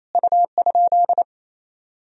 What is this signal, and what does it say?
Morse code, heard as a keyed tone, about 28 wpm, 700 Hz. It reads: U?
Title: Sentences from Top 100 Words and Abbreviations 28wpm